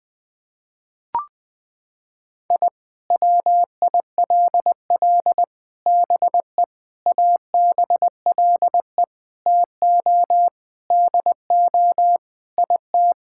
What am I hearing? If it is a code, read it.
Morse code, heard as a keyed tone, about 20 wpm, 700 Hz. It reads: I WILL BE ABLE TO DO IT
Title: Sentences with 4-Word Phrases - 20wpm